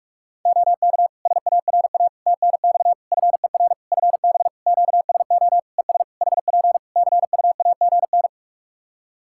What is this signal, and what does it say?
Morse code, heard as a keyed tone, about 41 wpm, 700 Hz. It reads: OK SARA TNX FER FB QSO ES HP CUAGN
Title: Practice QSO - Exchange 4 - Ending - 45wpm